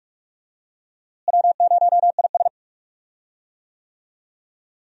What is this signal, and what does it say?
Morse code, heard as a keyed tone, about 46 wpm, 700 Hz. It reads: W0IS
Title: USA Call Signs - 4 Characters - Repeated 3x - 45wpm